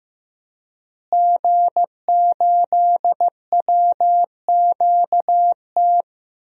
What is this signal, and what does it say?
Morse code, heard as a keyed tone, about 16 wpm, 700 Hz. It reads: G8WQT
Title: DXCC-100 Call Signs - 5 Characters - Repeated 2x - 15wpm